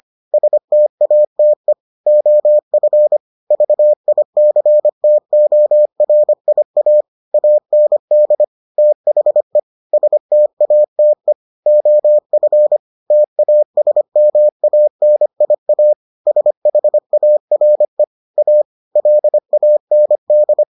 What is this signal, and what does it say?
Morse code, heard as a keyed tone, about 25 wpm, 600 Hz. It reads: STATE OF VICTORIA AND THE STATE OF TASMANIA SHARE A LAND